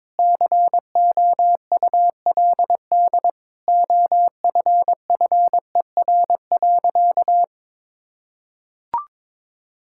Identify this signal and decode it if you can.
Morse code, heard as a keyed tone, about 22 wpm, 700 Hz. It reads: COULD OFFER.
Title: Sentences from Top 1600 Words 22wpm